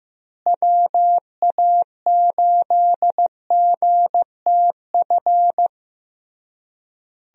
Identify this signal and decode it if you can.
Morse code, heard as a keyed tone, about 16 wpm, 700 Hz. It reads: WA8GTF
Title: USA Call Signs - 6 Characters - Repeated 3x - 15wpm